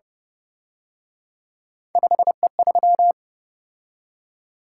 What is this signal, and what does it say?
Morse code, heard as a keyed tone, about 30 wpm, 700 Hz. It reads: 5E3